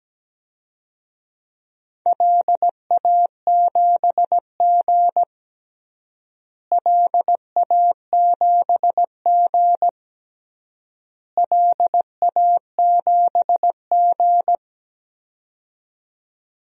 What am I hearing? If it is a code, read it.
Morse code, heard as a keyed tone, about 17 wpm, 700 Hz. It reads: LA7G LA7G LA7G
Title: DXCC-100 Call Signs - 4 Characters - Repeated 3x - 17wpm